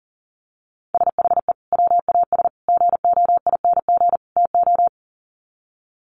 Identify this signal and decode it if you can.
Morse code, heard as a keyed tone, about 40 wpm, 700 Hz. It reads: SHE WAS GOING TO
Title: Sets of 4 Words 40wpm